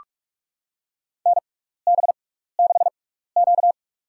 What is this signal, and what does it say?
Morse code, heard as a keyed tone, about 42 wpm, spaced as for 15 wpm, 700 Hz. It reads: NB6Q
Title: USA Call Signs - 4 Characters - 6x Character Spacing - 45wpm